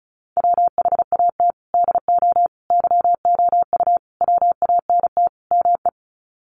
Morse code, heard as a keyed tone, about 34 wpm, 700 Hz. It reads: WHAT DO YOU WANT ME
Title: Sets of 5 Words Encore2 35wpm